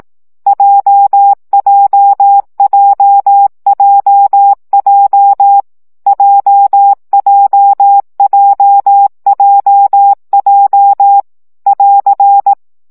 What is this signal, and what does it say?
Morse code, heard as a keyed tone, about 18 wpm, 800 Hz. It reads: JJJJJ JJJJJ <AR>
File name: CW gyakorlas j 15 baud